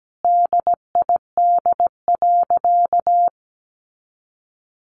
Morse code, heard as a keyed tone, about 17 wpm, 700 Hz. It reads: DID.
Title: Sentences from Top 500 Words 17wpm